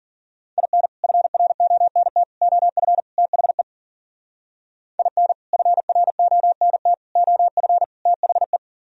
Morse code, heard as a keyed tone, about 50 wpm, 700 Hz. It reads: IN FRONT OF THE IN FRONT OF THE